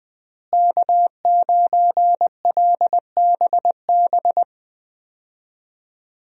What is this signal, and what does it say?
Morse code, heard as a keyed tone, about 20 wpm, 700 Hz. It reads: K9LBB
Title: USA Call Signs - 5 Characters - Repeated 2x - 20wpm